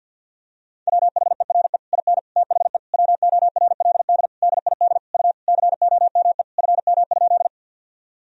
Morse code, heard as a keyed tone, about 52 wpm, 700 Hz. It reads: WHERE IN THE WORLD DID U COME FM?
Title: Sentences from Top 200 Words and Abbreviations 50wpm